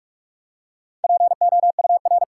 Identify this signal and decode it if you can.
Morse code, heard as a keyed tone, about 42 wpm, 700 Hz. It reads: POUR
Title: Four Letter Words - 45wpm